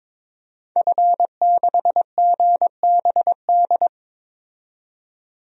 Morse code, heard as a keyed tone, about 22 wpm, 700 Hz. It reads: F6GBD
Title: DXCC-100 Call Signs - 5 Characters - 22wpm